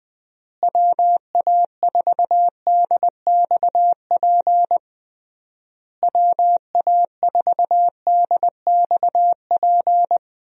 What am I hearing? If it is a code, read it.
Morse code, heard as a keyed tone, about 20 wpm, 700 Hz. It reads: WA4DXP WA4DXP
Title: USA Call Signs - 6 Characters - Repeated 3x - 20wpm